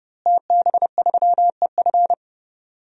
Morse code, heard as a keyed tone, about 31 wpm, 700 Hz. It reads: TB3EF
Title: DXCC-100 Call Signs - 5 Characters - Repeated 3x - 30wpm